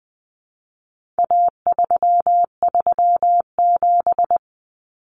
Morse code, heard as a keyed tone, about 20 wpm, 700 Hz. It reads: A337
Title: Common Cut Numbers - 4-Digit 20wpm